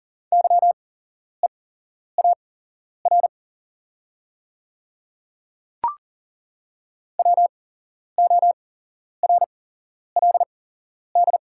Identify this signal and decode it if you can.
Morse code, heard as a keyed tone, about 40 wpm, spaced as for 11 wpm, 700 Hz. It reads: YEAR WORLD
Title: 8x Character Spacing - Top 500 Words - 40wpm